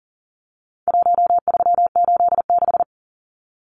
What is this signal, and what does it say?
Morse code, heard as a keyed tone, about 40 wpm, 700 Hz. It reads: 1386
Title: Numbers - Contest Style - 4-Digit 40wpm